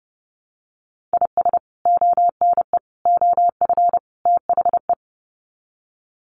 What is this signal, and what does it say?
Morse code, heard as a keyed tone, about 30 wpm, 700 Hz. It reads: IS ONE OF THE
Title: Sets of 4 Words 30wpm